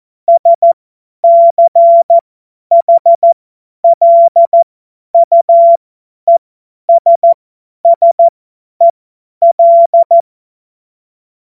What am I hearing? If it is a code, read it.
Morse code, heard as a keyed tone, about 14 wpm, spaced as for 10 wpm, 680 Hz. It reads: SCHLUESSEL
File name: Morsekurs / Witze / Witz 13 Mai 2025 50 F70